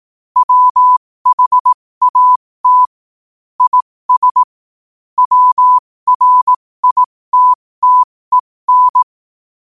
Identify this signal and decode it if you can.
Morse code, heard as a keyed tone, about 18 wpm, spaced as for 15 wpm, 1000 Hz. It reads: WHAT IS WRITTEN
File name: qotdfiles / 15wpm